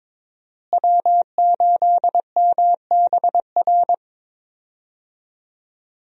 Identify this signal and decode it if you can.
Morse code, heard as a keyed tone, about 22 wpm, 700 Hz. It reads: W8MBR
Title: USA Call Signs - 5 Characters - Repeated 2x - 22wpm